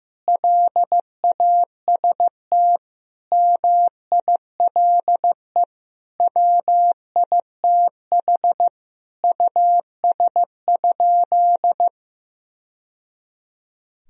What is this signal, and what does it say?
Morse code, heard as a keyed tone, about 15 wpm, 700 Hz. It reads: LAST MILE WITH US?
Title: Sentences from Top 1000 Words - Review Part 1 - 15wpm